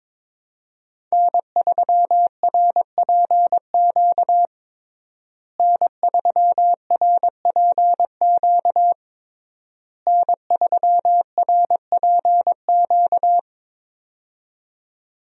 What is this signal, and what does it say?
Morse code, heard as a keyed tone, about 22 wpm, 700 Hz. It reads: N3RPQ N3RPQ N3RPQ
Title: USA Call Signs - 5 Characters - Repeated 3x - 22wpm